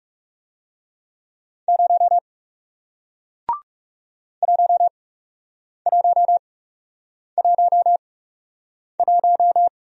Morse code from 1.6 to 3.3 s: 0

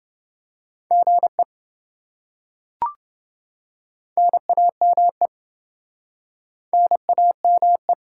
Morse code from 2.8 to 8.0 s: NAME NAME

